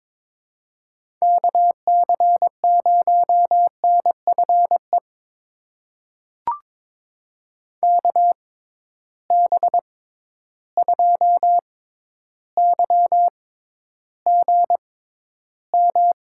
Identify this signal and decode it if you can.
Morse code, heard as a keyed tone, about 22 wpm, spaced as for 11 wpm, 700 Hz. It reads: KC0NFE KB2YGM
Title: USA Call Signs - 6 Characters - 6x Character Spacing - 22wpm